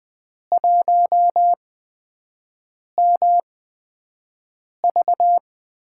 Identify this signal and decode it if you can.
Morse code, heard as a keyed tone, about 20 wpm, spaced as for 5 wpm, 700 Hz. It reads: 1MV
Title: USA Call Signs - 8x Character Spacing - 20wpm